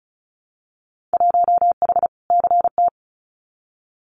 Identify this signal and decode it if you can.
Morse code, heard as a keyed tone, about 34 wpm, 700 Hz. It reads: 1H CT
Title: Winter Field Day Exchanges 35wpm